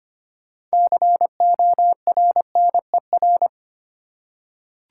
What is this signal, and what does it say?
Morse code, heard as a keyed tone, about 25 wpm, 700 Hz. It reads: CORNER
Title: Six Letter Words - 25wpm